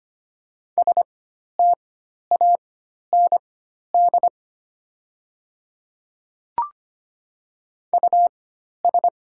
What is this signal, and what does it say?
Morse code, heard as a keyed tone, about 25 wpm, spaced as for 12 wpm, 690 Hz. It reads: STAND US
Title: 4x Character Spacing - Top 500 Words - 25wpm